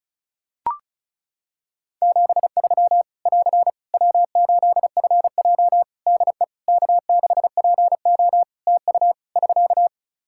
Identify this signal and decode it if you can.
Morse code, heard as a keyed tone, about 36 wpm, 700 Hz. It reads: 73 <AR> W8FJ DE K6PO TU <SK>
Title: Practice QSO - Exchange 4 - Ending - 35wpm